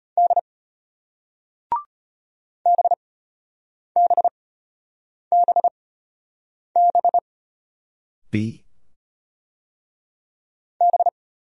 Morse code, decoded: D BBBB B

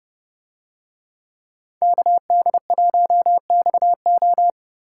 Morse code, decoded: KD1XO

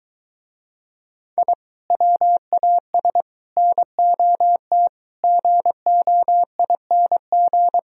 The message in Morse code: I WAS NOT GOING

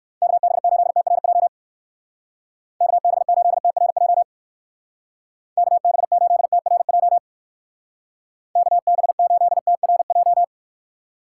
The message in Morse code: KB8TRJ KB8TRJ KB8TRJ KB8TRJ

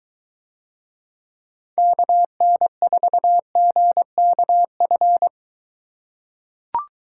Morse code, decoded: KN4GKF

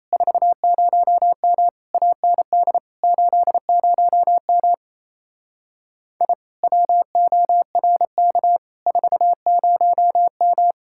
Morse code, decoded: 40M AND 80M I WORK 40M

